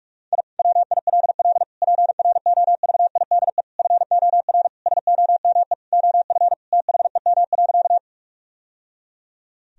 Morse code, heard as a keyed tone, about 46 wpm, 700 Hz. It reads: I WILL PROVIDE FOR SOME OF THEM.
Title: Sentences from Top 1000 Words - Review Part 3 - 45wpm